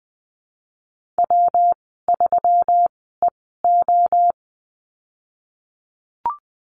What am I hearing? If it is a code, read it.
Morse code, heard as a keyed tone, about 20 wpm, spaced as for 14 wpm, 700 Hz. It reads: W3EO
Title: 122 Lesson 20g - Number 3 - Call Signs 20wpm